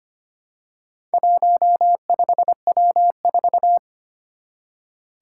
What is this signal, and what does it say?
Morse code, heard as a keyed tone, about 25 wpm, 700 Hz. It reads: 15W4
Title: Four Challenging Letter-Number Combinations 25wpm